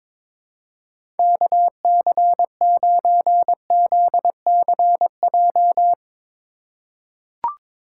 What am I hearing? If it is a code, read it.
Morse code, heard as a keyed tone, about 22 wpm, 700 Hz. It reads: KC9ZCJ
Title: USA Call Signs - 6 Characters - 2x Character Spacing - 22wpm